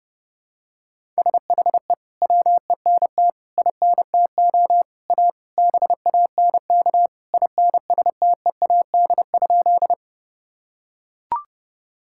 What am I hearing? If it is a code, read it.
Morse code, heard as a keyed tone, about 30 wpm, 700 Hz. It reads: SHE WENT INTO A BANK INSTEAD?
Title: Sentences from Top 1000 Words - Review Part 6 - 30wpm